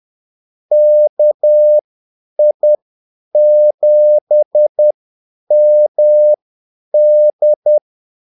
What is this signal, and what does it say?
Morse code, heard as a keyed tone, about 10 wpm, spaced as for 8 wpm, 600 Hz. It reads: KI7MD
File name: Callsigns-10-8wpm